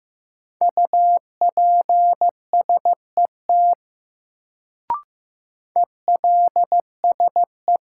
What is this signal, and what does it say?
Morse code, heard as a keyed tone, about 15 wpm, 700 Hz. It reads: UPSET ELSE
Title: Sentences from Top 800 Words - New Word Drill - 15wpm